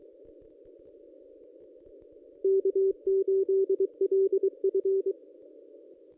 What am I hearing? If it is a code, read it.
Morse code, heard as a keyed tone, about 23 wpm, 390 Hz. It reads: K8LF
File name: contests / ss-cw-k5tr-op-n5rz -1200